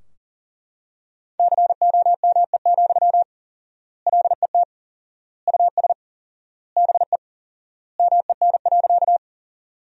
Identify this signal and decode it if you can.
Morse code, heard as a keyed tone, about 40 wpm, 700 Hz. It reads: COME, LET US BE MEN.
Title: 4x Word-Spacing - Sentences from Top 300 Words - 40wpm